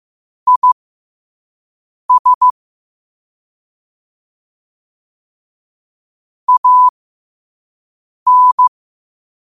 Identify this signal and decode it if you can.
Morse code, heard as a keyed tone, about 15 wpm, spaced as for 5 wpm, 1000 Hz. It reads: IS AN
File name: qotdfiles / 5wpm